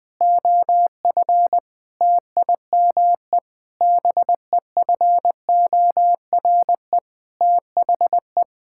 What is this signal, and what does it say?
Morse code, heard as a keyed tone, about 20 wpm, 700 Hz. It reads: OF TIME BEFORE THE
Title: Sets of 9 Words - 20wpm